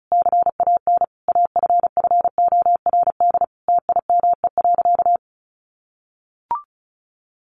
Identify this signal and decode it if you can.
Morse code, heard as a keyed tone, about 34 wpm, 700 Hz. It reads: CAN AFFORD TIME.
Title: Sentences from Top 1100 Words 35wpm